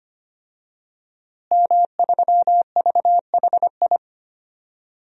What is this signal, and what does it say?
Morse code, heard as a keyed tone, about 25 wpm, 700 Hz. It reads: M3VHI
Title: DXCC-100 Call Signs - 5 Characters - Repeated 2x - 25wpm